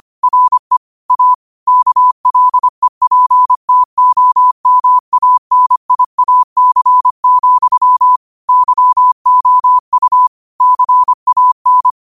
Morse code, heard as a keyed tone, about 25 wpm, 1000 Hz. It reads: RE A KLEPTOMANIAC, YOU CAN
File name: qotdfiles / 25wpm